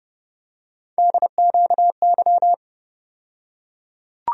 Morse code, heard as a keyed tone, about 29 wpm, 700 Hz. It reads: DQY
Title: USA Call Sign Postfixes - 3 Letters 30wpm